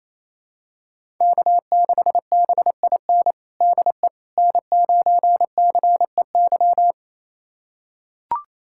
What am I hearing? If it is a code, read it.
Morse code, heard as a keyed tone, about 28 wpm, 700 Hz. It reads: K6BIN DE N9CEY
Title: USA Callsign DE Callsign - 5 Characters - 28wpm